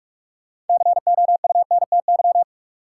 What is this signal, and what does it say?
Morse code, heard as a keyed tone, about 41 wpm, 700 Hz. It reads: COUNTY
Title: Sentences from Top 1200 Words - New Word Drill - 45wpm